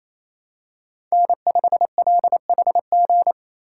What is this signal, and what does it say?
Morse code, heard as a keyed tone, about 28 wpm, 700 Hz. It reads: N5LHG
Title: USA Call Signs - 5 Characters - 2x Character Spacing - 28wpm